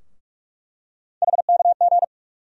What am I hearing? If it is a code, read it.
Morse code, heard as a keyed tone, about 42 wpm, 700 Hz. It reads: HKG
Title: USA Call Sign Postfixes - 3 Letters 45wpm